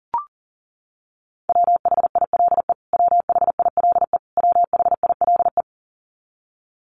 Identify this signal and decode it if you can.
Morse code, heard as a keyed tone, about 37 wpm, 700 Hz. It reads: WHILE WHILE WHILE
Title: Top 300 Words Repeated 40wpm